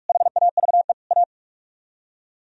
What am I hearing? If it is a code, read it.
Morse code, heard as a keyed tone, about 60 wpm, spaced as for 53 wpm, 700 Hz. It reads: HAVE A
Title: Speed-Racing - Sets of 4 Words 45wpm